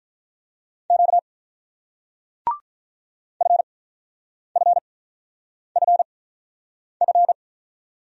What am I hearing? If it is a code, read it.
Morse code, heard as a keyed tone, about 47 wpm, spaced as for 9 wpm, 700 Hz. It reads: Q FFFF